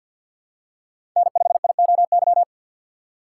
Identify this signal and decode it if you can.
Morse code, heard as a keyed tone, about 52 wpm, 690 Hz. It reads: N5IOY
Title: USA Call Signs - 5 Characters - 2x Character Spacing - 50wpm